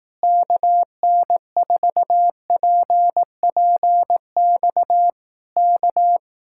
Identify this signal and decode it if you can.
Morse code, heard as a keyed tone, about 18 wpm, 700 Hz. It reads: KN4PPX K
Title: Speed-Racing Common QSO Phrases Encore - 15wpm